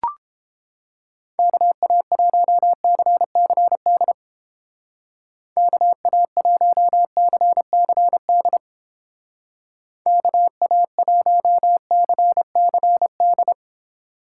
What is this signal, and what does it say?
Morse code, heard as a keyed tone, about 29 wpm, 700 Hz. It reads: KA1CCD KA1CCD KA1CCD